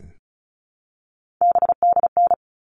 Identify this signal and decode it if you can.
Morse code, heard as a keyed tone, about 34 wpm, 700 Hz. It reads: BDN
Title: All Cut Numbers - 3-Digit 35wpm